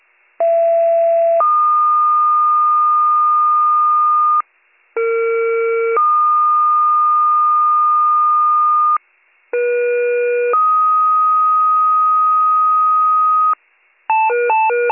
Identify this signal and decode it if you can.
Two-tone page, heard as a one-second tone, then a three-second tone, about 670 then 1220 Hz, about 460 then 1220 Hz, about 480 then 1250 Hz. It is heard from 0.4 to 13.5 s.